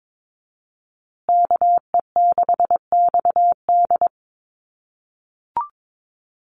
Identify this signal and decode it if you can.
Morse code, heard as a keyed tone, about 22 wpm, 700 Hz. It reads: KE6XD
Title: USA Call Signs Repeated 3x 22wpm